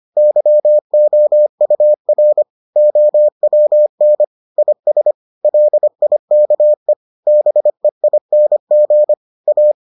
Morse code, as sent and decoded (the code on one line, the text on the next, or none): -.-- --- ..- .-. / --- .-- -. / .. ... / .-.. .. -.- . / -... .. -. --. / .-
YOUR OWN IS LIKE BEING A